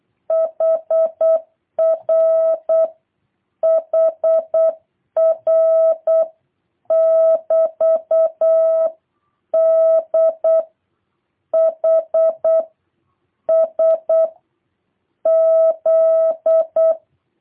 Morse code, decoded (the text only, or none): HR HR <BT> D H S Z